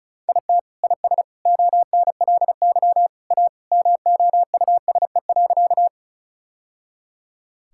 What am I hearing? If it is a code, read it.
Morse code, heard as a keyed tone, about 36 wpm, 700 Hz. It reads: IT IS ONLY A MOUSE.
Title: Sentences from Top 1000 Words - Review Part 5 - 35wpm